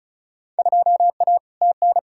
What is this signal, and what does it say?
Morse code, heard as a keyed tone, about 36 wpm, 690 Hz. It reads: 2A TN
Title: Field Day Exchanges Encore - 35wpm